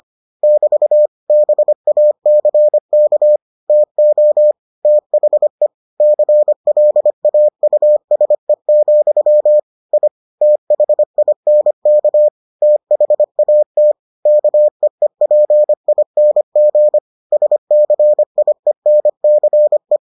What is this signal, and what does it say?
Morse code, heard as a keyed tone, about 25 wpm, 600 Hz. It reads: <BT> BACK TO THE CLAUSE, I THINK THAT KEEPING SCIENCE